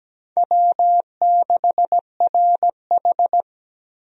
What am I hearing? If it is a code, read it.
Morse code, heard as a keyed tone, about 17 wpm, 700 Hz. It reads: W6RH
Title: USA Call Signs - 4 Characters - Repeated 3x - 17wpm